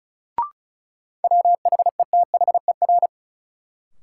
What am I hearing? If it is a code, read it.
Morse code, heard as a keyed tone, about 36 wpm, 700 Hz. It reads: WHETHER
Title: Sentences from Top 1000 Words - Review Part 3 - New Word Drill - 35wpm